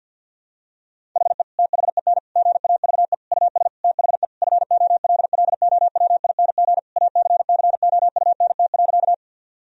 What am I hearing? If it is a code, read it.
Morse code, heard as a keyed tone, about 50 wpm, 700 Hz. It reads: HE THEN GAVE US THE FOLLOWING ACCOUNT.
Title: Sentences from Top 1000 Words - Review Part 4 - 50wpm